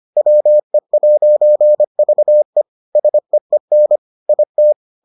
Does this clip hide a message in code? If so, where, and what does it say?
Yes: Morse code from 0.2 to 4.7 s, heard as a keyed tone, about 25 wpm, 600 Hz. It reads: WE'VE SEEN IT